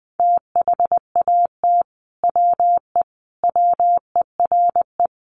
Morse code, heard as a keyed tone, about 20 wpm, 700 Hz. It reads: THAT WE WERE
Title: Sets of 5 Words Encore2 20wpm